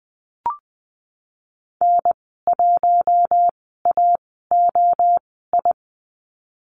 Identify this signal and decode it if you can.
Morse code, heard as a keyed tone, about 19 wpm, spaced as for 14 wpm, 700 Hz. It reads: N1AOI
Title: 048 Lesson 9g - Number 4 - Call Signs 20wpm